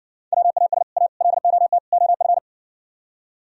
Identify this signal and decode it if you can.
Morse code, heard as a keyed tone, about 59 wpm, 700 Hz. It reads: WAS A LOT OF